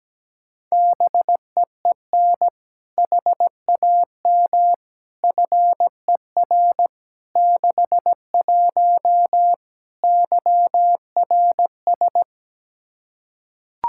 Morse code, heard as a keyed tone, about 17 wpm, 700 Hz. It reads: BEEN HAM FER 61 YRS
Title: Practice QSO - Exchange 3 - More Info - 17wpm